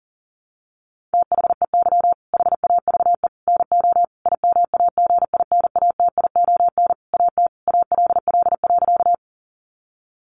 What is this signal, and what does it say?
Morse code, heard as a keyed tone, about 40 wpm, 700 Hz. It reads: THEY HAVE NO IMAGINATION AT ALL.
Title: Sentences from Top 1300 Words 40wpm